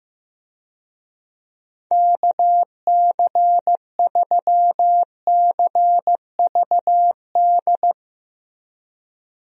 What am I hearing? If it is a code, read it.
Morse code, heard as a keyed tone, about 16 wpm, 700 Hz. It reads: KC3CVD
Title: USA Call Signs - 6 Characters - Repeated 3x - 15wpm